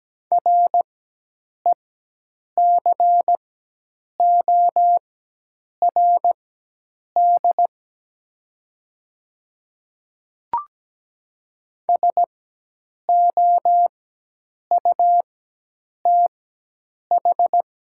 Morse code, heard as a keyed tone, about 17 wpm, spaced as for 8 wpm, 700 Hz. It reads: RECORD SOUTH
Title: 4x Character Spacing - Top 500 Words - 17wpm